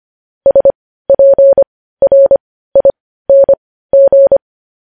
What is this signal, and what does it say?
Morse code, heard as a keyed tone, about 25 wpm, spaced as for 15 wpm, 550 Hz. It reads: SPRING
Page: News Headlines in Morse Code at 15 WPM